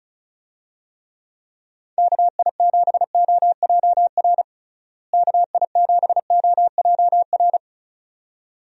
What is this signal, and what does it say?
Morse code, heard as a keyed tone, about 34 wpm, 700 Hz. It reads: KI7OJR KI7OJR
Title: USA Call Signs - 6 Characters - Repeated 2x - 35wpm